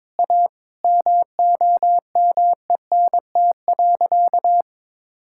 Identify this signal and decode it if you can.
Morse code, heard as a keyed tone, about 22 wpm, 700 Hz. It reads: A MOMENT.
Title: Sentences from Top 1600 Words 22wpm